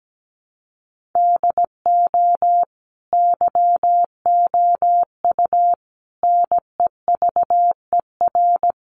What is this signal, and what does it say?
Morse code, heard as a keyed tone, about 17 wpm, 700 Hz. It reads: DO YOU NEVER
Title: Sentences from Top 1500 Words 17wpm